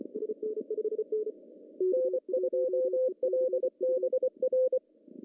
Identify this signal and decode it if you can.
Morse code, heard as a keyed tone, about 23 wpm, 510 Hz. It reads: N2LBR
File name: CW / cqww-cw-k5tr